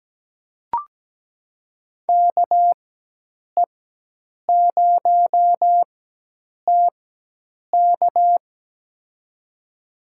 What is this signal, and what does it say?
Morse code, heard as a keyed tone, about 17 wpm, spaced as for 8 wpm, 700 Hz. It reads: KE0TK
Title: USA Call Signs - 5 Characters - 4x Character Spacing - 17wpm